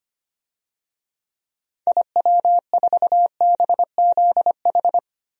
Morse code, heard as a keyed tone, about 25 wpm, 700 Hz. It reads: IW4BZH
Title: DXCC-100 Call Signs - 6 Characters - 25wpm